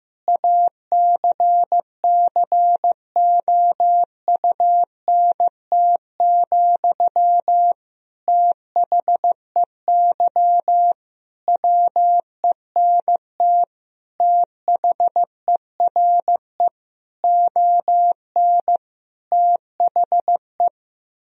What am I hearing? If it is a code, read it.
Morse code, heard as a keyed tone, about 15 wpm, 700 Hz. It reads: ACCOUNT, THEY WENT THERE ON THE
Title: Sentences from Top 1000 Words - Review Part 4 - 15wpm